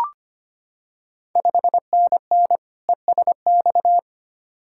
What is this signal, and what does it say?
Morse code, heard as a keyed tone, about 25 wpm, 700 Hz. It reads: 5NN ESX
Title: Ontario QSO Party Exchange 25wpm